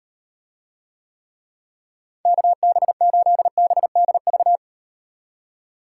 Morse code, heard as a keyed tone, about 36 wpm, 700 Hz. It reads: KB8BDV